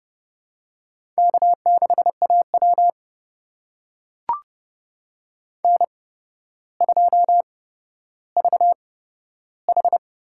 Morse code, heard as a keyed tone, about 30 wpm, spaced as for 13 wpm, 700 Hz. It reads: K6AW N2VH
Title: USA Call Signs - 4 Characters - 8x Character Spacing - 30wpm